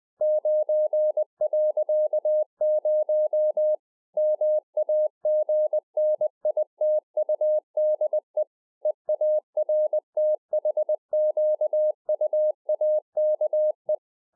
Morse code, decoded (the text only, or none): 9.0 MAGNITUDE EARTHQUAKE